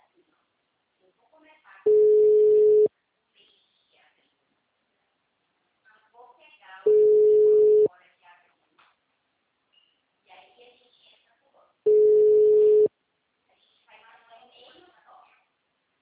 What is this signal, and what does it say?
Ringback tone (Europe, 425 Hz), with speech.